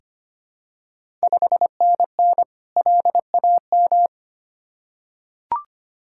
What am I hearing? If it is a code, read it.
Morse code, heard as a keyed tone, about 25 wpm, 690 Hz. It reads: 5NN LAM
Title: Ontario QSO Party Exchange 25wpm